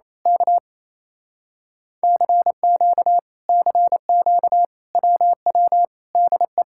Morse code, heard as a keyed tone, about 27 wpm, 700 Hz. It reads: K CQ CQ WW DE
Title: Speed-Racing Common QSO Phrases - 28wpm